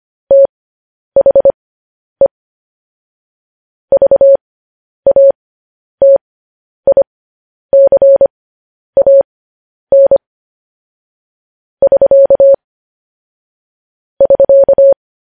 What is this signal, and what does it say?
Morse code, heard as a keyed tone, about 25 wpm, spaced as for 10 wpm, 550 Hz. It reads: THE VATICAN <SK> <SK>
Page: News Headlines in Morse Code at 10 WPM